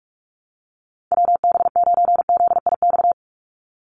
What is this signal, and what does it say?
Morse code, heard as a keyed tone, about 44 wpm, 700 Hz. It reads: WB9ZIX
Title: USA Call Signs 45wpm